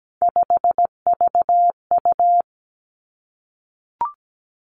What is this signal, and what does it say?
Morse code, heard as a keyed tone, about 17 wpm, 700 Hz. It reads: 5VU
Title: Three Challenging Letter-Number Combinations 17wpm